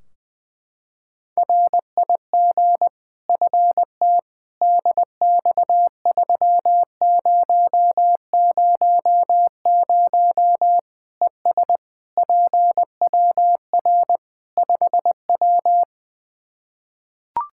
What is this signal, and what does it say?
Morse code, heard as a keyed tone, about 20 wpm, 700 Hz. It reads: RIG FT DX3000 ES PWR 5W
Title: Practice QSO - Exchange 2 - Info - 20wpm